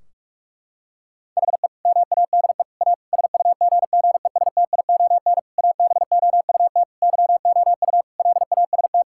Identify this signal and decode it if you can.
Morse code, heard as a keyed tone, about 46 wpm, 700 Hz. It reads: HE MADE A SUGGESTION ABOUT YOU LAST
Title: Sentences from Top 1600 Words 45wpm